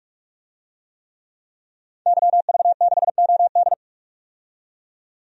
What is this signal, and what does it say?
Morse code, heard as a keyed tone, about 45 wpm, 700 Hz. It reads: YV6OD
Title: DXCC-100 Call Signs - 5 Characters - 45wpm